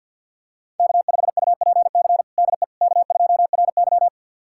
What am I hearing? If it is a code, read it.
Morse code, heard as a keyed tone, about 50 wpm, 700 Hz. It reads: K5UPC DE K2RY
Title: USA Callsign DE Callsign 50wpm